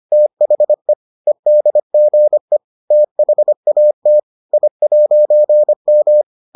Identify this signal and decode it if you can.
Morse code, heard as a keyed tone, about 25 wpm, 600 Hz. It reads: THE EDGE THAT I'M